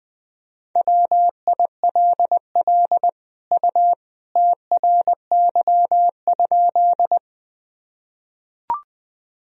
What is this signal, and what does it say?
Morse code, heard as a keyed tone, about 20 wpm, 700 Hz. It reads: WILL U TRY?
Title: Sentences from Top 200 Words and Abbreviations 20wpm